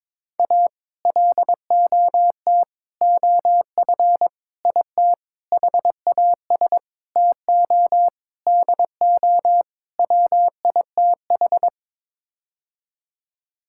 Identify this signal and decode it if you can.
Morse code, heard as a keyed tone, about 22 wpm, 700 Hz. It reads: A LOT OF IT HAS TO DO WITH
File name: practice / 22wpm / Sets of 8 Words - 22wpm